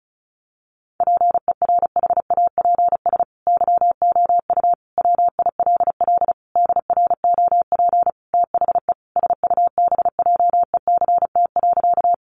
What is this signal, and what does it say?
Morse code, heard as a keyed tone, about 34 wpm, 700 Hz. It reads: PERHAPS YOU WILL DROP THE SUBJECT.
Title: Sentences from Top 1400 Words 35wpm